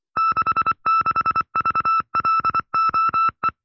Morse code, heard as a keyed tone, about 24 wpm, 1310 Hz. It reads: 66VLOE
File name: FreeFiles / CW-practice / 20WPM-side2